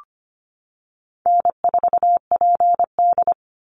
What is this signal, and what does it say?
Morse code, heard as a keyed tone, about 25 wpm, 700 Hz. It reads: N4PD
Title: Easy Call Signs - 25wpm